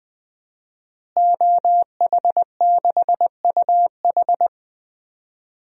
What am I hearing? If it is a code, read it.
Morse code, heard as a keyed tone, about 20 wpm, 700 Hz. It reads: OH6UH
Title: DXCC-100 Call Signs - 5 Characters - Repeated 2x - 20wpm